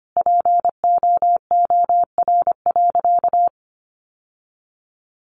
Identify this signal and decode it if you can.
Morse code, heard as a keyed tone, about 25 wpm, 700 Hz. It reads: POOR.
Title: Sentences from Top 1100 Words 25wpm